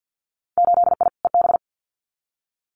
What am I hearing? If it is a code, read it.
Morse code, heard as a keyed tone, about 50 wpm, 700 Hz. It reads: 8I EB
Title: Winter Field Day Exchanges Encore 50wpm